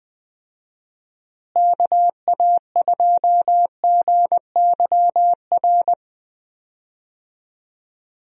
Morse code, heard as a keyed tone, about 20 wpm, 700 Hz. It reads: KA2GYR
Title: USA Call Signs - 6 Characters - Repeated 3x - 20wpm